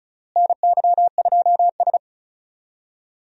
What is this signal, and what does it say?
Morse code, heard as a keyed tone, about 36 wpm, 700 Hz. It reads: NY2S
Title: USA Call Signs - 4 Characters - Repeated 2x - 35wpm